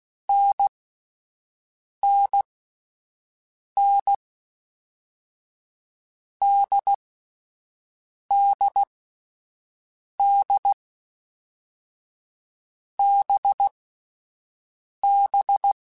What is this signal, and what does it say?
Morse code, heard as a keyed tone, about 16 wpm, 800 Hz. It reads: N N N D D D B B
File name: Difficult Combos / Difficult Combos-mp3s / NDB NDB6 @10